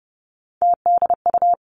Morse code, heard as a keyed tone, about 30 wpm, 700 Hz. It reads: TDU
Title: Three Challenging Letter-Number Combinations 30wpm